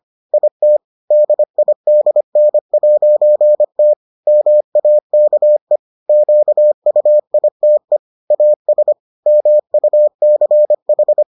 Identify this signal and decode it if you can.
Morse code, heard as a keyed tone, about 25 wpm, 600 Hz. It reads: IT DIDN'T MAKE QUITE AS MUCH